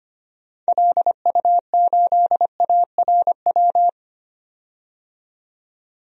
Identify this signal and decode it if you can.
Morse code, heard as a keyed tone, about 25 wpm, 700 Hz. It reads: LU8ARW
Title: DXCC-100 Call Signs - 6 Characters - 25wpm